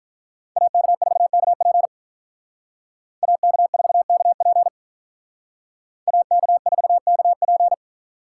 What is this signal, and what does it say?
Morse code, heard as a keyed tone, about 45 wpm, 700 Hz. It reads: AK4KP AK4KP AK4KP